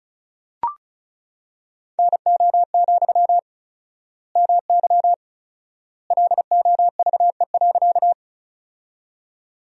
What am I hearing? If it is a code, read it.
Morse code, heard as a keyed tone, about 36 wpm, 700 Hz. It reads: NO, MY LOVE.
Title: 4x Word-Spacing - Sentences from Top 400 Words - 35wpm